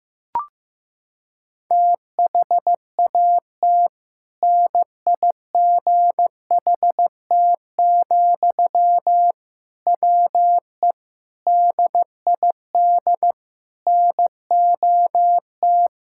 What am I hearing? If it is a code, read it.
Morse code, heard as a keyed tone, about 15 wpm, 700 Hz. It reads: THAT NIGHT, WE DID NOT
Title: Sentences from Top 1000 Words - Review Part 6 - 15wpm